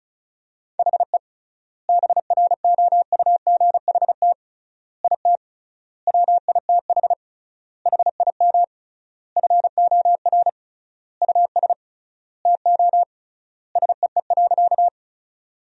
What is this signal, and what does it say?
Morse code, heard as a keyed tone, about 36 wpm, 700 Hz. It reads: HE BROUGHT IT WITH HIM FOR US TO SEE.
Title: 3x Word-Spacing - Sentences from Top 500 Words - 35wpm